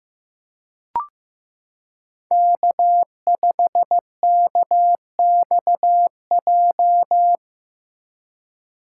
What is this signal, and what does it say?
Morse code, heard as a keyed tone, about 15 wpm, 700 Hz. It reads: K5KXJ
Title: K1USN SST QSO Practice - 15wpm